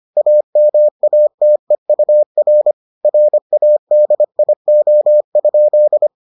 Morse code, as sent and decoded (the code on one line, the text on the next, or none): .- -- .- - . ..- .-. / .-. .- -.. .. --- ..--..
AMATEUR RADIO?